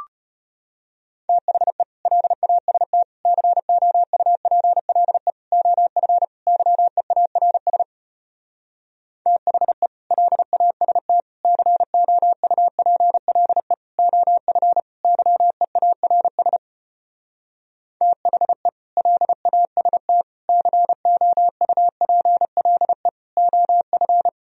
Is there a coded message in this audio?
Yes: Morse code from 1.3 to 24.3 s, heard as a keyed tone, about 33 wpm, 700 Hz. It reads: THE LAST COUPLE OF YEARS THE LAST COUPLE OF YEARS THE LAST COUPLE OF